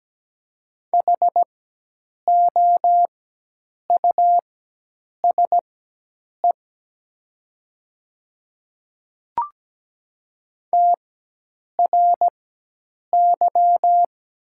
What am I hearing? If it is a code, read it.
Morse code, heard as a keyed tone, about 17 wpm, spaced as for 8 wpm, 700 Hz. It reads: HOUSE TRY
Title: 4x Character Spacing - Top 200 Words - 17wpm